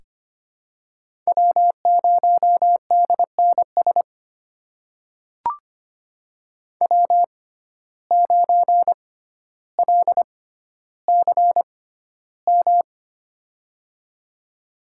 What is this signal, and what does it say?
Morse code, heard as a keyed tone, about 25 wpm, spaced as for 13 wpm, 700 Hz. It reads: W0DNS W9LCM
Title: USA Call Signs - 5 Characters - 6x Character Spacing - 25wpm